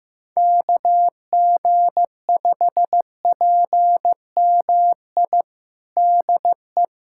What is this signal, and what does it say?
Morse code, heard as a keyed tone, about 15 wpm, 700 Hz. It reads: KG5PMI DE
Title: USA Callsign DE Callsign - 6 Characters - 15wpm